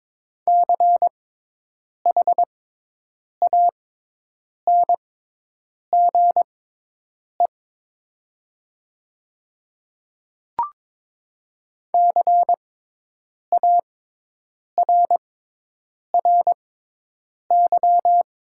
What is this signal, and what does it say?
Morse code, heard as a keyed tone, about 22 wpm, spaced as for 8 wpm, 700 Hz. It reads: CHANGE CARRY
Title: 6x Character Spacing - Top 300 Words - 22wpm